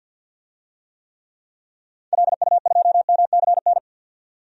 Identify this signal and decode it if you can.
Morse code, heard as a keyed tone, about 51 wpm, 700 Hz. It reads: LU2MCN
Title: DXCC-100 Call Signs - 6 Characters - 50wpm